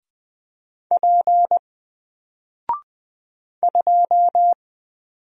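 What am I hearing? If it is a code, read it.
Morse code, heard as a keyed tone, about 20 wpm, 700 Hz. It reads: P 2
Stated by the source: Single letter and number Morse drill at 20 wpm, from N4LCD.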